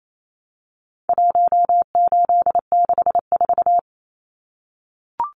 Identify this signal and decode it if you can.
Morse code, heard as a keyed tone, about 28 wpm, 700 Hz. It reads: 1864
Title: Numbers - 4-Digit 28wpm